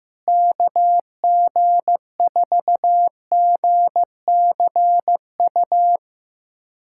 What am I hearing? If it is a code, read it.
Morse code, heard as a keyed tone, about 15 wpm, 700 Hz. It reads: KG4GCU
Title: USA Callsign DE Callsign 15wpm